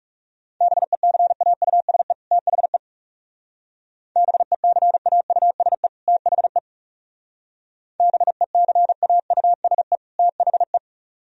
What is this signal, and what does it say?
Morse code, heard as a keyed tone, about 36 wpm, 700 Hz. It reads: BECAUSE THE BECAUSE THE BECAUSE THE
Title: Speed-Racing - Sets of 2 Words 30wpm